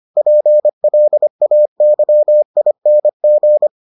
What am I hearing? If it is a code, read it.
Morse code, heard as a keyed tone, about 25 wpm, 600 Hz. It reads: PLAYING